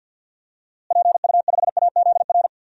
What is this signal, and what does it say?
Morse code, heard as a keyed tone, about 49 wpm, 700 Hz. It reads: PU5AZR
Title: DXCC-100 Call Signs - 6 Characters - Repeated 3x - 50wpm